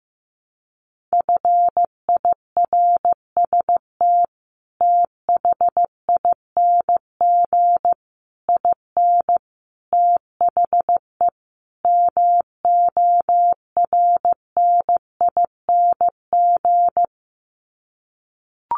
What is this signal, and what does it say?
Morse code, heard as a keyed tone, about 15 wpm, 700 Hz. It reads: FIRST THING IN THE MORNING
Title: Sets of 5 Words Encore2 15wpm